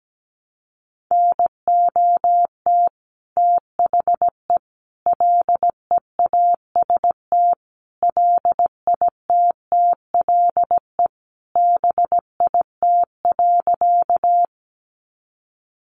Morse code, heard as a keyed tone, about 17 wpm, 700 Hz. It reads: NOT THE LEAST LITTLE BIT.
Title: Sentences from Top 900 Words 17wpm